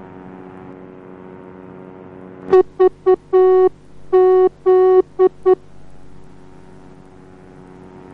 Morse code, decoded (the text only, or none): VZ